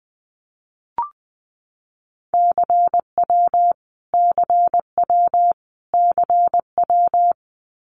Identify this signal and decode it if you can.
Morse code, heard as a keyed tone, about 20 wpm, 700 Hz. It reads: CW CW CW
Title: Abbreviations Repeated 3x - 20wpm